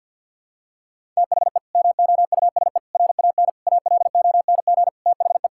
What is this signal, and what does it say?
Morse code, heard as a keyed tone, about 48 wpm, 700 Hz. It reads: THE MOUSE RAN ALONG THE
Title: Sentences from Top 1000 Words - Review Part 5 - 50wpm